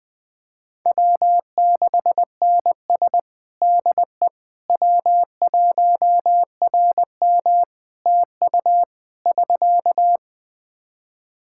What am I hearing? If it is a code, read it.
Morse code, heard as a keyed tone, about 20 wpm, 700 Hz. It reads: W6NS DE W1RM TU <SK>
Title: Practice QSO - Exchange 4 - Ending - 20wpm